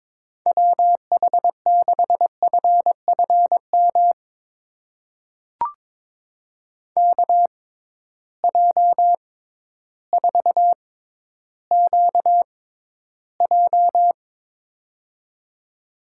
Morse code, heard as a keyed tone, about 22 wpm, spaced as for 12 wpm, 700 Hz. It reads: WH6FFM KJ4QJ
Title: USA Call Signs - 6x Character Spacing - 22wpm